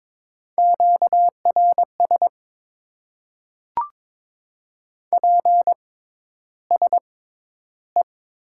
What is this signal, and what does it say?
Morse code, heard as a keyed tone, about 21 wpm, spaced as for 11 wpm, 700 Hz. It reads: QRS PSE
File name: Top 90 QSO Elements - 6x Character Spacing - 22wpm